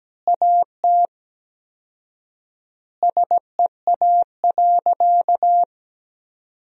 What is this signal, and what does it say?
Morse code, heard as a keyed tone, about 17 wpm, 700 Hz. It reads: AT SEA.
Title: 4x Word-Spacing - Sentences from Top 300 Words - 17wpm